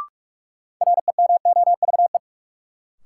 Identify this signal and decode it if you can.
Morse code, heard as a keyed tone, about 43 wpm, 700 Hz. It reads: REMOVE
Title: Sentences from Top 800 Words - New Word Drill - 45wpm